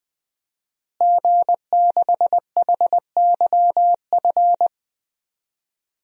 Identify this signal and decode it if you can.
Morse code, heard as a keyed tone, about 20 wpm, 700 Hz. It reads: G6HYF
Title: DXCC-100 Call Signs - 5 Characters - Repeated 3x - 20wpm